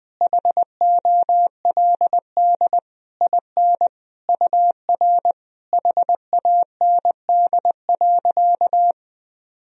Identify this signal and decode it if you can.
Morse code, heard as a keyed tone, about 20 wpm, 700 Hz. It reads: HOLD IN UR HAND.
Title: Sentences from Top 400 Words and Abbreviations 20wpm